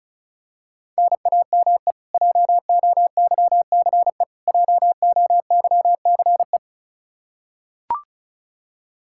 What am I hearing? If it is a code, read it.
Morse code, heard as a keyed tone, about 36 wpm, 700 Hz. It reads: NAME JOYCE JOYCE
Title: Practice QSO - Exchange 1 - Beginning - 35wpm